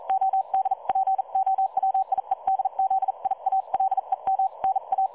Morse code, heard as a keyed tone, about 40 wpm, 770 Hz. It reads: ODPOWIEDZIALEM NA
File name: dlazaawansowanych / nagraniazpasma / websdr recording 3565.0kHz